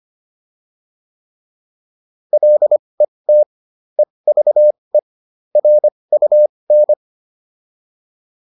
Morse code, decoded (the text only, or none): LET EVE RUN